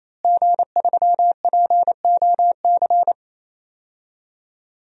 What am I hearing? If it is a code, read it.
Morse code, heard as a keyed tone, about 28 wpm, 700 Hz. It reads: G3POC
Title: DXCC-100 Call Signs - 5 Characters - Repeated 3x - 28wpm